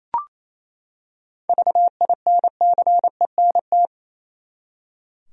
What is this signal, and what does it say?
Morse code, heard as a keyed tone, about 28 wpm, 700 Hz. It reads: VINCENT
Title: USA Common Names - 28wpm